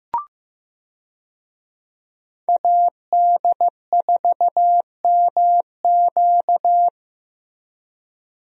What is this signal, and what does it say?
Morse code, heard as a keyed tone, about 15 wpm, 700 Hz. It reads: AD4MQ
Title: USA Call Signs - 5 Characters - Repeated 3x - 15wpm